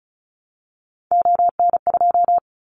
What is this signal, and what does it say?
Morse code, heard as a keyed tone, about 36 wpm, 690 Hz. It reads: ON2
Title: Callsign Prefixes for Top 50 DXCC - 35wpm